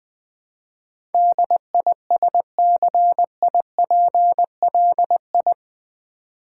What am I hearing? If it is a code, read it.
Morse code, heard as a keyed tone, about 20 wpm, 700 Hz. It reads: DISCIPLI
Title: Build A Word - 4 Letters At A Time - 10-Letter Words - 20wpm